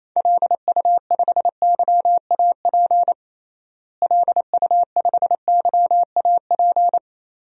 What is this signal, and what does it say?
Morse code, heard as a keyed tone, about 28 wpm, 700 Hz. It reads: LU5YAP LU5YAP
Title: DXCC-100 Call Signs - 6 Characters - Repeated 3x - 28wpm